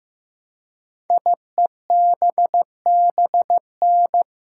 Morse code, heard as a keyed tone, about 15 wpm, 700 Hz. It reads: IEBBN
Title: Code Groups - 5 Letters - 15wpm